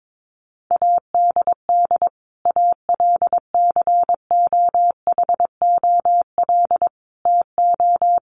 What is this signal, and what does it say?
Morse code, heard as a keyed tone, about 22 wpm, 700 Hz. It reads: ADD ALCOHOL TO
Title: Sentences from Top 1500 Words 22wpm